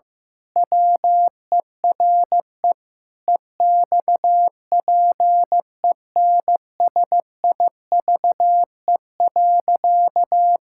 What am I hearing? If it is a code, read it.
Morse code, heard as a keyed tone, about 15 wpm, 700 Hz. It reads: WERE EXPENSIVE.
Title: Pangram Challenge v2 - 15wpm